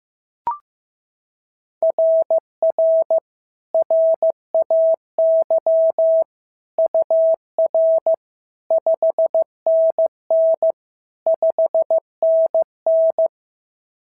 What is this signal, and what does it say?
Morse code, heard as a keyed tone, about 14 wpm, 650 Hz. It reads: RR RAY UR 5NN 5NN
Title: SOTA QSO Practice 15wpm